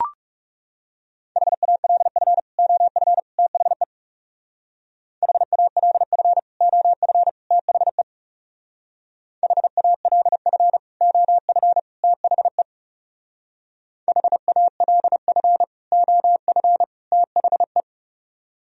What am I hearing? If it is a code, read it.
Morse code, heard as a keyed tone, about 36 wpm, 700 Hz. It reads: HALF OF THE HALF OF THE HALF OF THE HALF OF THE
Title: Speed-Racing - Sets of 3 Words 30wpm